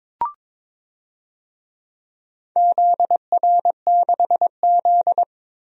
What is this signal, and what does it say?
Morse code, heard as a keyed tone, about 21 wpm, 700 Hz. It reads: ZR6Z
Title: DXCC-100 Call Signs - 4 Characters - Repeated 2x - 22wpm